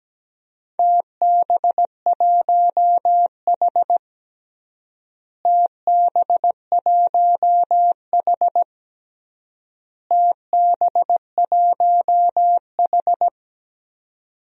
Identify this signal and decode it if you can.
Morse code, heard as a keyed tone, about 17 wpm, 700 Hz. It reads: TB1H TB1H TB1H
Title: DXCC-100 Call Signs - 4 Characters - Repeated 3x - 17wpm